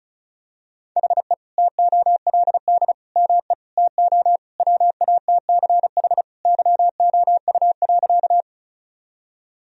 Morse code, heard as a keyed tone, about 36 wpm, 700 Hz. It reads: HE TOLD ME TO WATCH YOU.
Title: Sentences from Top 1000 Words - Review Part 1 - 35wpm